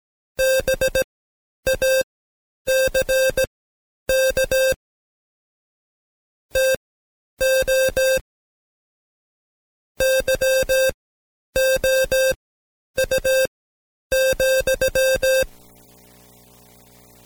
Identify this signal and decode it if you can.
Morse code, heard as a keyed tone, about 17 wpm, spaced as for 9 wpm, 530 Hz. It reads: BACK TO YOU,
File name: Code Quick / AudioDownld2 / Seg131-Stepper4 9WPM